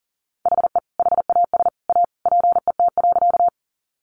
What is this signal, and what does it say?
Morse code, heard as a keyed tone, about 40 wpm, 700 Hz. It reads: HE HAS A PET.
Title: Easy Phrases 40wpm